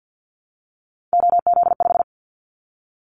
Morse code, heard as a keyed tone, about 50 wpm, 700 Hz. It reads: OZ5
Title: Callsign Prefixes for Top 100 DXCC - 50wpm